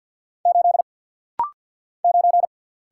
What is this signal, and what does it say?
Morse code, heard as a keyed tone, about 49 wpm, 700 Hz. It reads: 8 9
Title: Single Letter-Number - Rapid-Fire 50wpm